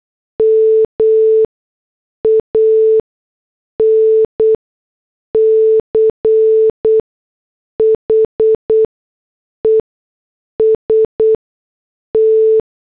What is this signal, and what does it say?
Morse code, heard as a keyed tone, about 8 wpm, spaced as for 6 wpm, 440 Hz. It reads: MANCHEST